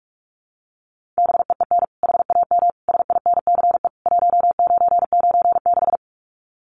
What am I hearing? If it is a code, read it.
Morse code, heard as a keyed tone, about 44 wpm, 700 Hz. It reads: BEEN HAM SINCE 1996
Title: Common QSO Phrases 45wpm